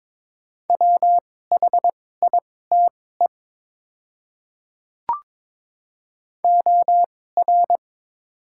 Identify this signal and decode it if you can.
Morse code, heard as a keyed tone, about 22 wpm, spaced as for 16 wpm, 700 Hz. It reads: WHITE OR
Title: 2x Character Spacing - Top 300 Words - 22wpm